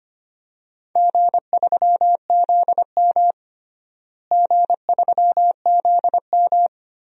Morse code, heard as a keyed tone, about 25 wpm, 700 Hz. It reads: G3ZM G3ZM
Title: DXCC-100 Call Signs - 4 Characters - Repeated 3x - 25wpm